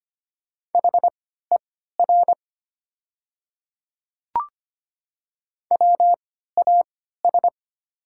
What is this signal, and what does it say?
Morse code, heard as a keyed tone, about 25 wpm, spaced as for 14 wpm, 700 Hz. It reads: HER WAS
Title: 3x Character Spacing - Top 100 Words - 25wpm